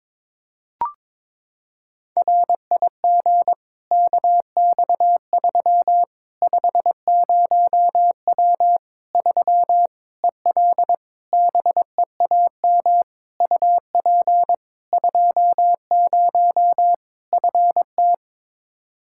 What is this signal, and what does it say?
Morse code, heard as a keyed tone, about 22 wpm, 700 Hz. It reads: RIG KX3 50W 3 EL BEAM UP 20 FT
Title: Practice QSO - Exchange 2 - Info - 22wpm